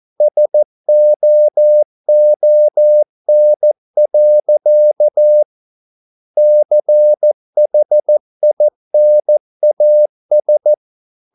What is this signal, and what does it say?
Morse code, heard as a keyed tone, about 14 wpm, 600 Hz. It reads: SOON. CHINAS